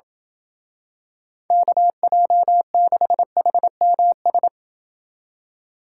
Morse code, heard as a keyed tone, about 28 wpm, 700 Hz. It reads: KJ6HMS